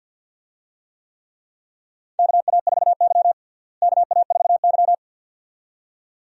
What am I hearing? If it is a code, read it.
Morse code, heard as a keyed tone, about 51 wpm, 700 Hz. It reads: KA4Y KA4Y
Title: USA Call Signs - 4 Characters - Repeated 2x - 50wpm